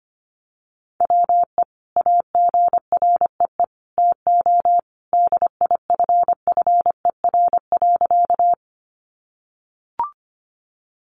Morse code, heard as a keyed tone, about 25 wpm, 700 Hz. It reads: WE AGREE TO DIFFER.
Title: Sentences from Top 1100 Words 25wpm